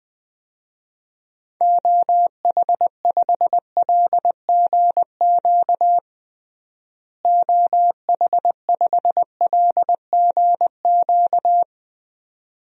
Morse code, heard as a keyed tone, about 21 wpm, 700 Hz. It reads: OH5LGQ OH5LGQ
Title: DXCC-100 Call Signs - 6 Characters - Repeated 2x - 20wpm